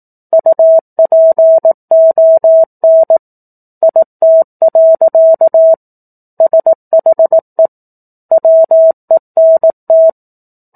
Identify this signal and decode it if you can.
Morse code, heard as a keyed tone, about 18 wpm, 650 Hz. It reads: UPON IT. SHE WENT
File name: 04-Chapter-04 18 WPM